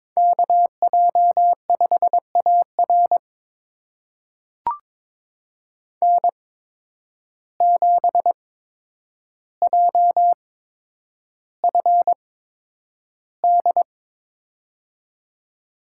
Morse code, heard as a keyed tone, about 22 wpm, spaced as for 9 wpm, 700 Hz. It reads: KJ5AR N7JFD
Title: USA Call Signs - 5 Characters - 8x Character Spacing - 22wpm